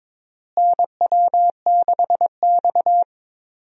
Morse code, heard as a keyed tone, about 22 wpm, 700 Hz. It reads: NW6X
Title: USA Call Signs - 4 Characters - Repeated 2x - 22wpm